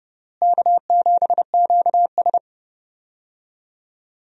Morse code, heard as a keyed tone, about 30 wpm, 700 Hz. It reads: K7QS
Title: USA Call Signs - 4 Characters - Repeated 3x - 30wpm